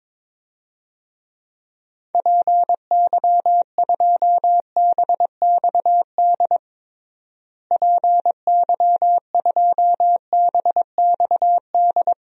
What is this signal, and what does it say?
Morse code, heard as a keyed tone, about 22 wpm, 700 Hz. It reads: PY2BXD PY2BXD
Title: DXCC-100 Call Signs - 6 Characters - Repeated 2x - 22wpm